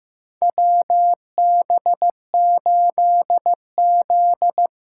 Morse code, heard as a keyed tone, about 15 wpm, 700 Hz. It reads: WB8Z
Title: USA Call Signs - 6 Characters - 6x Character Spacing - 15wpm